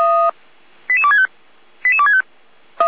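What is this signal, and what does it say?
Five-tone selective call (ZVEI), heard as quick runs of five tones, tones of about 70 ms. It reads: 88265 88265
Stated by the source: ZVEI five-tone selective call 88365 received by SDR.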